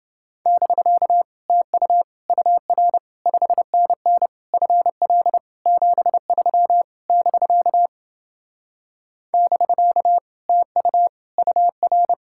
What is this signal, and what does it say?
Morse code, heard as a keyed tone, about 29 wpm, 700 Hz. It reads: <BK> TU UR 5NN FL 73 <BK> <BK> TU UR